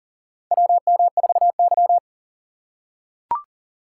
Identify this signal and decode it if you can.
Morse code, heard as a keyed tone, about 40 wpm, 700 Hz. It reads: WM4Y
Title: USA Call Signs - 2x Character Spacing - 40wpm